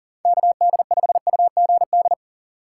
Morse code, heard as a keyed tone, about 38 wpm, 700 Hz. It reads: KD5UGD
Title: USA Call Signs - 6x Character Spacing - 40wpm